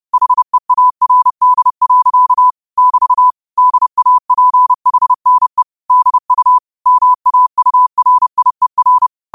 Morse code, heard as a keyed tone, about 30 wpm, 1000 Hz. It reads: HEARD. <BT> DAPHNE DU MAURIER
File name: qotdfiles / 30wpm